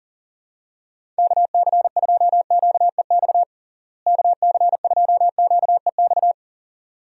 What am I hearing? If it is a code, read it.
Morse code, heard as a keyed tone, about 40 wpm, 700 Hz. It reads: KC2QEX KC2QEX
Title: USA Call Signs - 6 Characters - Repeated 2x - 40wpm